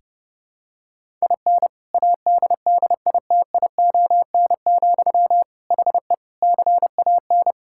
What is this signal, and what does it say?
Morse code, heard as a keyed tone, about 30 wpm, 700 Hz. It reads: IN ADDITION, HE CAN
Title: Sentences from Top 1000 Words - Review Part 6 - 30wpm